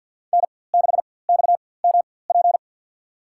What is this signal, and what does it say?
Morse code, heard as a keyed tone, about 51 wpm, spaced as for 23 wpm, 700 Hz. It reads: N6XMP
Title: USA Call Signs - 5 Characters - 4x Character Spacing - 50wpm